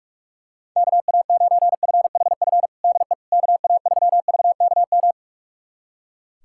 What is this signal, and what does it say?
Morse code, heard as a keyed tone, about 42 wpm, 700 Hz. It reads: KA9FHF DE KA3VKM
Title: USA Callsign DE Callsign - 6 Characters - 45wpm